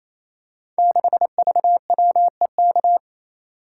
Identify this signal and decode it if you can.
Morse code, heard as a keyed tone, about 28 wpm, 700 Hz. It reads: 6VWEK
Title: Code Groups - 5 Letter:Numbers - 28wpm